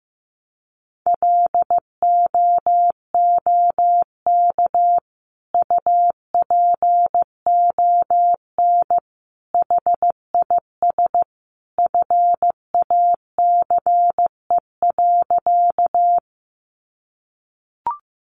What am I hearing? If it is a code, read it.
Morse code, heard as a keyed tone, about 15 wpm, 700 Hz. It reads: LOOK UPON HIS FACE.
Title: Sentences from Top 1100 Words 15wpm